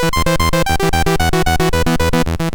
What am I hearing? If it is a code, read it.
There is no code.